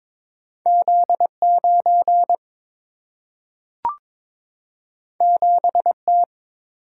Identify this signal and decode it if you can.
Morse code, heard as a keyed tone, about 22 wpm, 700 Hz. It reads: Z 9 7 T
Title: Single-Letter Single-Number 22wpm